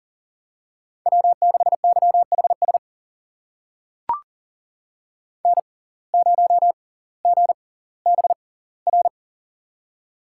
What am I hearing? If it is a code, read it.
Morse code, heard as a keyed tone, about 39 wpm, spaced as for 20 wpm, 690 Hz. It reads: W6YHS N0GBR
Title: USA Call Signs - 5 Characters - 6x Character Spacing - 40wpm